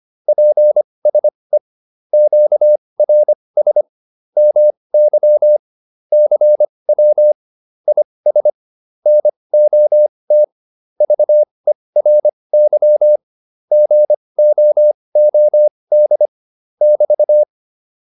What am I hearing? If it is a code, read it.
Morse code, heard as a keyed tone, about 25 wpm, spaced as for 20 wpm, 600 Hz. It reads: PSE QRS MY CW IS NOT VERY GOOD <BT>